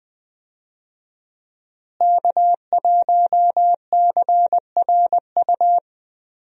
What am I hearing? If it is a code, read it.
Morse code, heard as a keyed tone, about 20 wpm, 700 Hz. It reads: K1CRU
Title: USA Call Signs - 5 Characters - Repeated 3x - 20wpm